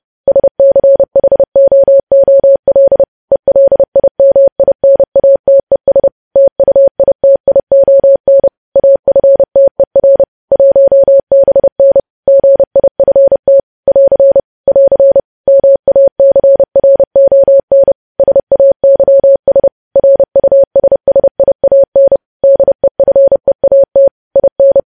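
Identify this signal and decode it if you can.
Morse code, heard as a keyed tone, about 30 wpm, 550 Hz. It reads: SCHOOL ELIMINATES TUITION AFTER 1BN GIFT <AR> <AR> MACRON SAYS RUSSIAN DEFEAT IN
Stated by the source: News headlines sent in Morse at 30 WPM.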